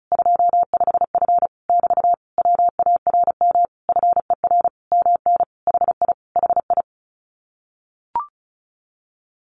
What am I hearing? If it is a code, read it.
Morse code, heard as a keyed tone, about 34 wpm, 700 Hz. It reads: 25F <BT> WARM FER MN HI HI
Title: Common QSO Phrases Encore 35wpm